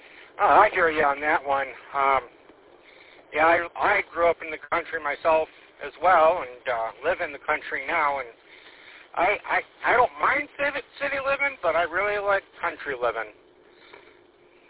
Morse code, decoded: none

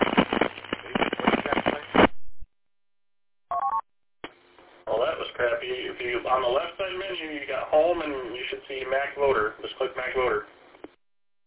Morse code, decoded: none

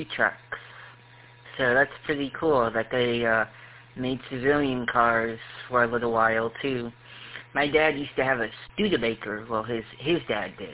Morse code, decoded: none

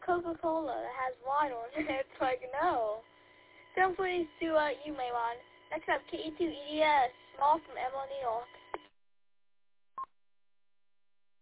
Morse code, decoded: none

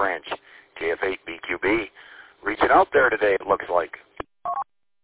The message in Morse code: none

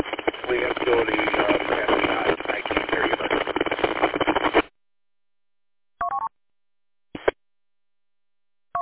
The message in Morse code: none

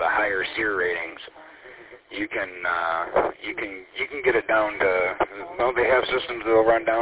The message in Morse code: none

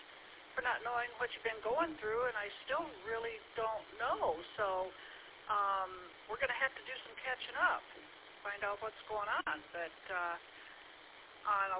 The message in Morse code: none

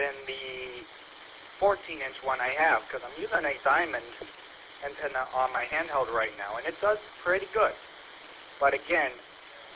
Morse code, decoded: none

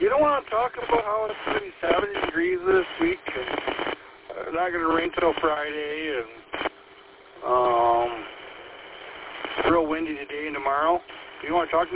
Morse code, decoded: none

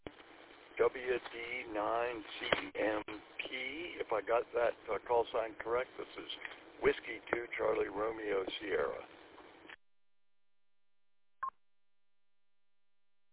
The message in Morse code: none

